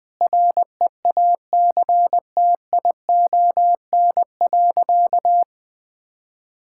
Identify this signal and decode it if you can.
Morse code, heard as a keyed tone, about 20 wpm, 700 Hz. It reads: REACTION.
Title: Sentences with 8-Word Phrases - 20wpm